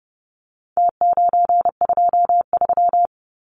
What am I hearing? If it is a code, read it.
Morse code, heard as a keyed tone, about 30 wpm, 700 Hz. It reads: T923
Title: Numbers - Contest Style - 4-Digit 30wpm